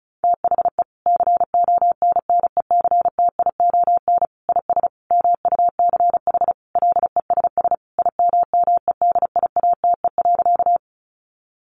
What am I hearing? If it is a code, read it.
Morse code, heard as a keyed tone, about 34 wpm, 700 Hz. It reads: THE CONNECTION IS MUCH LESS IMMEDIATE.
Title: Sentences from Top 1300 Words 35wpm